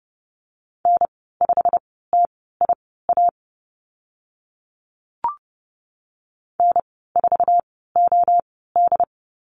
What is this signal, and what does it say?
Morse code, heard as a keyed tone, about 30 wpm, spaced as for 17 wpm, 700 Hz. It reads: N5TIA N4OD
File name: Lesson 15g - Number 5 - Call Signs 30wpm